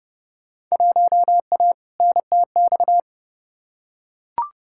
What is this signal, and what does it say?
Morse code, heard as a keyed tone, about 30 wpm, 700 Hz. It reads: 1A NTX
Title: Field Day Exchanges - 30wpm